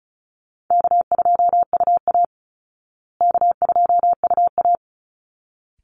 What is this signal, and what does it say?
Morse code, heard as a keyed tone, about 34 wpm, 700 Hz. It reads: K2UA K2UA
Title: Call Signs Repeated 3x - 35wpm